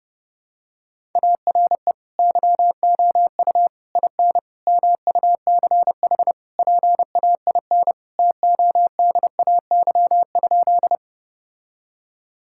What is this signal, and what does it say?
Morse code, heard as a keyed tone, about 30 wpm, 700 Hz. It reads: ARE YOU IN MUCH PAIN TODAY?
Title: Sentences from Top 1600 Words 30wpm